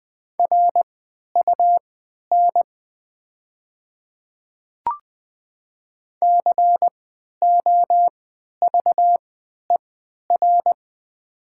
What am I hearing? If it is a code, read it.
Morse code, heard as a keyed tone, about 20 wpm, spaced as for 11 wpm, 700 Hz. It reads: RUN COVER
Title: 3x Character Spacing - Top 300 Words - 20wpm